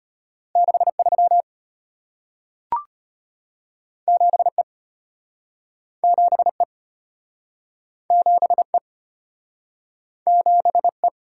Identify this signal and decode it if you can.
Morse code, heard as a keyed tone, about 34 wpm, 700 Hz. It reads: 63 7E 7E 7E 7E